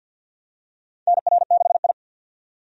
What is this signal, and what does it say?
Morse code, heard as a keyed tone, about 53 wpm, 700 Hz. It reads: NR6I
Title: USA Call Signs - 4 Characters - 50wpm